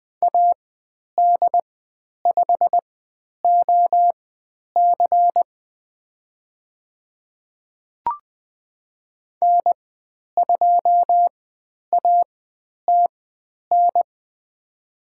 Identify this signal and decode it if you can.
Morse code, heard as a keyed tone, about 20 wpm, spaced as for 10 wpm, 700 Hz. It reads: AD5OC N2ATN
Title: 109 Lesson 18g - Letter m - Call Signs 20wpm